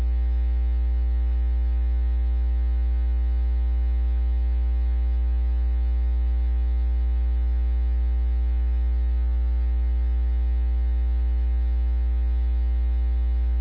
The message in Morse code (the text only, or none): none